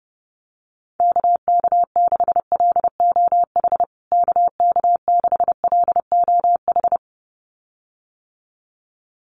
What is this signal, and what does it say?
Morse code, heard as a keyed tone, about 30 wpm, 700 Hz. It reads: KK6LOH KK6LOH
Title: USA Call Signs Repeated 2x 30wpm